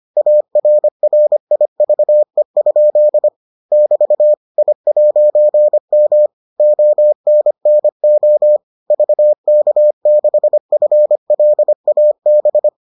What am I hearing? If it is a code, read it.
Morse code, heard as a keyed tone, about 25 wpm, 600 Hz. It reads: ARRIVE? <BT> I'M ONNO VK6FLAB